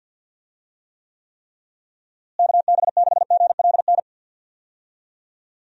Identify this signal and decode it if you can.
Morse code, heard as a keyed tone, about 50 wpm, 700 Hz. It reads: KB6GLN
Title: USA Call Signs - 6 Characters - Repeated 3x - 50wpm